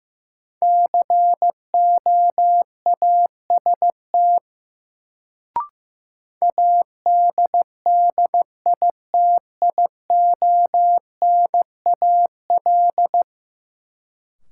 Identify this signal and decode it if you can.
Morse code, heard as a keyed tone, about 15 wpm, 700 Hz. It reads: COAST ADDITIONAL
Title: Sentences from Top 1000 Words - Review Part 4 - New Word Drill - 15wpm